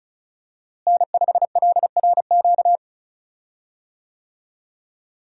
Morse code, heard as a keyed tone, about 36 wpm, 700 Hz. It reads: N5LRQ
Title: USA Call Signs - 5 Characters - Repeated 2x - 35wpm